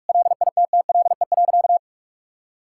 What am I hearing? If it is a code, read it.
Morse code, heard as a keyed tone, about 46 wpm, 700 Hz. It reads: LITTLE.
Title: Sentences from Top 1000 Words - Review Part 5 - 45wpm